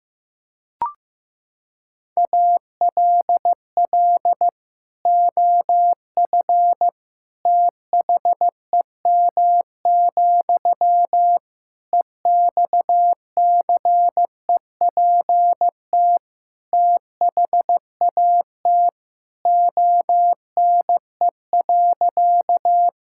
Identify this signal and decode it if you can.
Morse code, heard as a keyed tone, about 16 wpm, 700 Hz. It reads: ALL OF THEM, EXCEPT THAT ONE.
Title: Sentences from Top 1600 Words 15wpm